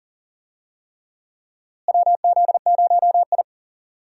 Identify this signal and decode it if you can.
Morse code, heard as a keyed tone, about 39 wpm, 700 Hz. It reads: WZ0I
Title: USA Call Signs - 4 Characters - Repeated 3x - 40wpm